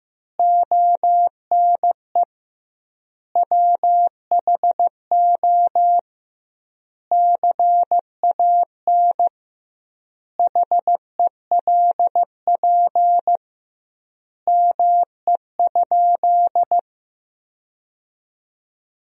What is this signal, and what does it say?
Morse code, heard as a keyed tone, about 16 wpm, 700 Hz. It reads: ONE WHO CAN HELP ME?
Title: 2x Word-Spacing - Sentences from Top 200 Words - 15wpm